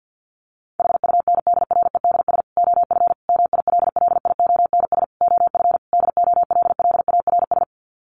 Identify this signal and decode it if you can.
Morse code, heard as a keyed tone, about 50 wpm, 700 Hz. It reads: HUNDREDS OF MILLIONS OF DOLLARS
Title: Sets of 5 Words 50wpm